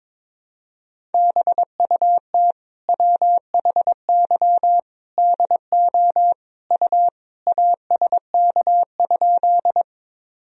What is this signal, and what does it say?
Morse code, heard as a keyed tone, about 22 wpm, 700 Hz. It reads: BUT WHY DO U ASK?
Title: Sentences from Top 200 Words and Abbreviations 22wpm